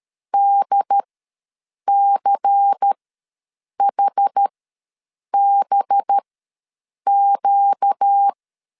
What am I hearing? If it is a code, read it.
Morse code, heard as a keyed tone, about 13 wpm, spaced as for 7 wpm, 800 Hz. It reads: DCHBQ
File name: LEZIONI / 6 PKX0 / 6 PK ws 12wpm 3116